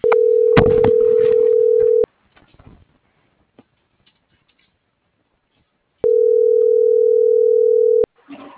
Ringback tone (North America), 440 + 480 Hz.